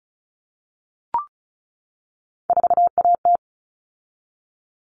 Morse code, heard as a keyed tone, about 34 wpm, 700 Hz. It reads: 4AT
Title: All Cut Numbers - 3-Digit 35wpm